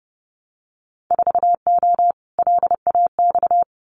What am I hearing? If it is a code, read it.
Morse code, heard as a keyed tone, about 30 wpm, 700 Hz. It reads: 4O LAX
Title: Winter Field Day Exchanges Encore 30wpm